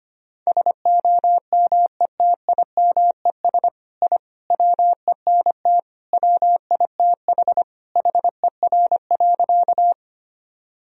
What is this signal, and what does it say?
Morse code, heard as a keyed tone, about 25 wpm, 700 Hz. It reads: SOMETIMES I WENT WITH HER.
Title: Sentences from Top 1000 Words - Review Part 3 - 25wpm